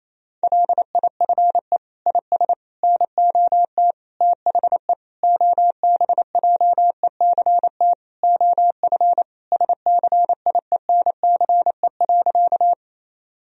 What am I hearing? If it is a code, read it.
Morse code, heard as a keyed tone, about 27 wpm, 700 Hz. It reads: LIFE IS NOT THE OBJECT OF SCIENCE.
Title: Sentences from Top 1000 Words - Review Part 3 - 28wpm